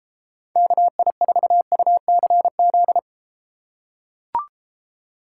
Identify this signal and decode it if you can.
Morse code, heard as a keyed tone, about 32 wpm, 700 Hz. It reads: KI4UCZ